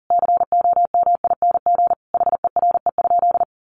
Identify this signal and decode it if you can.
Morse code, heard as a keyed tone, about 40 wpm, 700 Hz. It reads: COMING HERE?
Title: Sentences from Top 1300 Words 40wpm